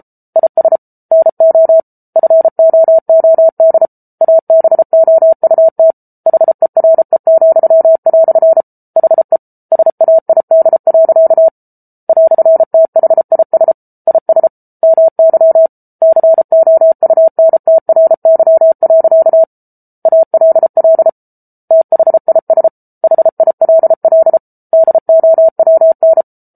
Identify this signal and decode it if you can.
Morse code, heard as a keyed tone, about 34 wpm, 650 Hz. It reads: IS NO FOOD ABOUT HERE," HE SAID. "THIS IS MY COUNTRY. ALL THIS HILL DOWN